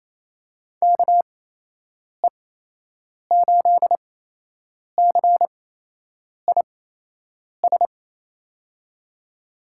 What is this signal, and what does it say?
Morse code, heard as a keyed tone, about 28 wpm, spaced as for 8 wpm, 700 Hz. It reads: KE8CIS
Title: USA Call Signs - 6 Characters - 8x Character Spacing - 28wpm